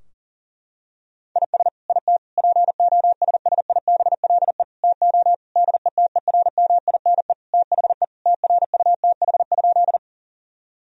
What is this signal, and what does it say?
Morse code, heard as a keyed tone, about 37 wpm, 700 Hz. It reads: IS IT POSSIBLE TO DETERMINE THE TRUTH?
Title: Sentences from Top 1000 Words - Review Part 4 - 40wpm